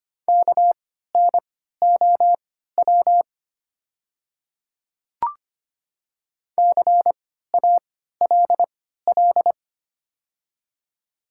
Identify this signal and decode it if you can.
Morse code, heard as a keyed tone, about 25 wpm, spaced as for 14 wpm, 700 Hz. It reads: KNOW CALL
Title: 3x Character Spacing - Top 100 Words - 25wpm